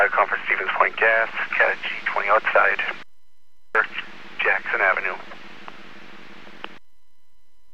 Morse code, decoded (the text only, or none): none